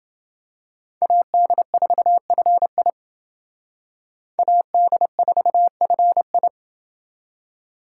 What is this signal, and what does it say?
Morse code, heard as a keyed tone, about 28 wpm, 700 Hz. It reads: AD4FI AD4FI